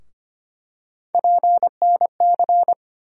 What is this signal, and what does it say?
Morse code, heard as a keyed tone, about 25 wpm, 700 Hz. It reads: PNC
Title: USA Call Sign Postfixes - 3 Letters 25wpm